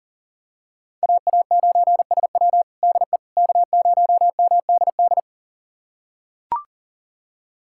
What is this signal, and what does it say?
Morse code, heard as a keyed tone, about 40 wpm, 700 Hz. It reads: AA9SW DE K0MDD
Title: USA Callsign DE Callsign 40wpm